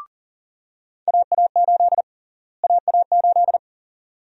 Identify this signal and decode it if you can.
Morse code, heard as a keyed tone, about 38 wpm, 700 Hz. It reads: AA8 AA8
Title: USA Call Sign Prefixes - Repeated 2x - 40wpm